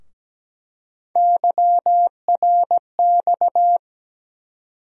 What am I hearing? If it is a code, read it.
Morse code, heard as a keyed tone, about 17 wpm, 700 Hz. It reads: YRX
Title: USA Call Sign Postfixes - 3 Letters 17wpm